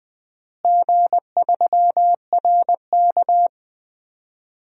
Morse code, heard as a keyed tone, about 19 wpm, 700 Hz. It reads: G3RK
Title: DXCC-100 Call Signs - 4 Characters - Repeated 2x - 20wpm